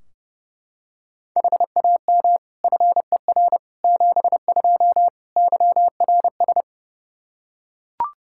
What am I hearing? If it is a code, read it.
Morse code, heard as a keyed tone, about 30 wpm, 700 Hz. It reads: HAM FER 72 YRS
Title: K1USN SST QSO Practice 30wpm